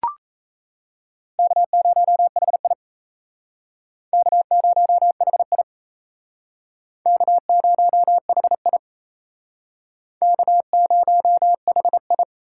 Morse code, heard as a keyed tone, about 35 wpm, 700 Hz. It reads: K0HI K0HI K0HI K0HI
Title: Speed Racing - USA 1x2 Call Signs 28wpm